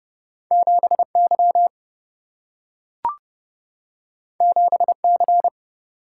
Morse code, heard as a keyed tone, about 30 wpm, 700 Hz. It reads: 7Y 7C
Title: Single-Letter Single-Number 30wpm